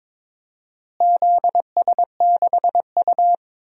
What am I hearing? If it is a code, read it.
Morse code, heard as a keyed tone, about 22 wpm, 700 Hz. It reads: ZS6U
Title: DXCC-100 Call Signs - 4 Characters - Repeated 2x - 22wpm